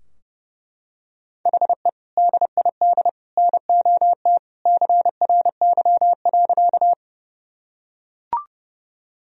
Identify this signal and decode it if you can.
Morse code, heard as a keyed tone, about 32 wpm, 700 Hz. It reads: HE DID NOT CRY.
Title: Sentences from Top 1000 Words - Review Part 2 - 30wpm